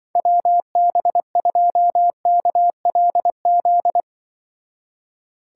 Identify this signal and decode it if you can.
Morse code, heard as a keyed tone, about 24 wpm, 700 Hz. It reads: WB2KLZ
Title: Speed Racing - USA 2x3 Call Signs 20wpm